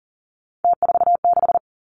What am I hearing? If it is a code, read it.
Morse code, heard as a keyed tone, about 38 wpm, 700 Hz. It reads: T46
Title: Three Challenging Letter-Number Combinations 40wpm